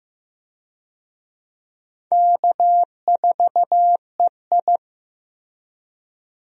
Morse code, heard as a keyed tone, about 15 wpm, 700 Hz. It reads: K4EI
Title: USA Call Signs - 4 Characters - Repeated 3x - 15wpm